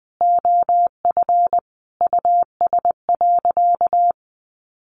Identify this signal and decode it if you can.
Morse code, heard as a keyed tone, about 20 wpm, 700 Hz. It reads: OF US.
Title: Sentences from Top 1300 Words 20wpm